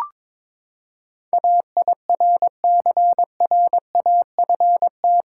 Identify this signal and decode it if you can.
Morse code, heard as a keyed tone, about 22 wpm, 700 Hz. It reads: AIRCRAFT
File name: practice / 22wpm / Eight Letter Words - 22wpm